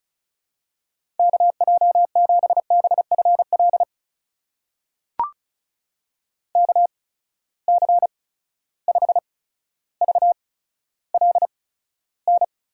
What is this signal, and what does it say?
Morse code, heard as a keyed tone, about 36 wpm, spaced as for 15 wpm, 700 Hz. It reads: KJ7BFL KC5VLN
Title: USA Call Signs - 6 Characters - 8x Character Spacing - 35wpm